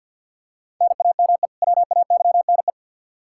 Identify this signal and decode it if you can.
Morse code, heard as a keyed tone, about 52 wpm, 700 Hz. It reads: NAME WAYNE
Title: Practice QSO - Exchange 1 - Beginning - 50wpm